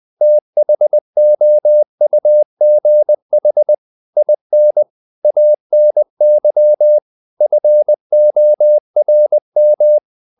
Morse code, heard as a keyed tone, about 20 wpm, 600 Hz. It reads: THOUGH IN ANY FORM
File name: the old man and the sea part 2 20wpm